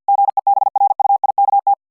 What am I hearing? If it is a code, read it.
Morse code, heard as a keyed tone, about 50 wpm, 800 Hz. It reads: GEBRUIKT